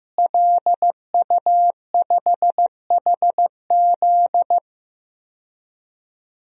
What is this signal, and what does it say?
Morse code, heard as a keyed tone, about 16 wpm, 700 Hz. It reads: LU5HZ
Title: DXCC-100 Call Signs - 5 Characters - Repeated 2x - 15wpm